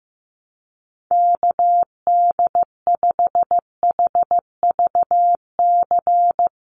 Morse code, heard as a keyed tone, about 15 wpm, 700 Hz. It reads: KD5HVC
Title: USA Call Signs 15wpm